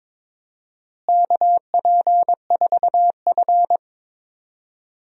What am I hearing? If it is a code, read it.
Morse code, heard as a keyed tone, about 23 wpm, 700 Hz. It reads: KP4F
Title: DXCC-100 Call Signs - 4 Characters - Repeated 2x - 22wpm